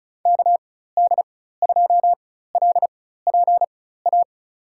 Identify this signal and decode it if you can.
Morse code, heard as a keyed tone, about 36 wpm, spaced as for 16 wpm, 700 Hz. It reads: KD2LPA
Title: USA Call Signs - 6 Characters - 4x Character Spacing - 35wpm